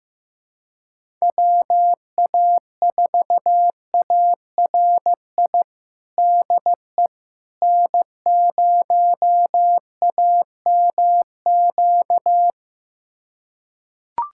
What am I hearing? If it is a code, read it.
Morse code, heard as a keyed tone, about 15 wpm, 700 Hz. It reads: WA4ARI DE N0AMQ
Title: USA Callsign DE Callsign 15wpm